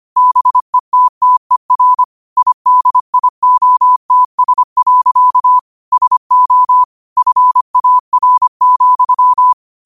Morse code, heard as a keyed tone, about 25 wpm, 1000 Hz. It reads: BETTER IDIOTS. SO FAR,
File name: qotdfiles / 25wpm